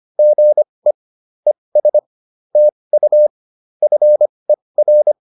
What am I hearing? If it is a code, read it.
Morse code, heard as a keyed tone, about 25 wpm, spaced as for 20 wpm, 600 Hz. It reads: GE ES TU FER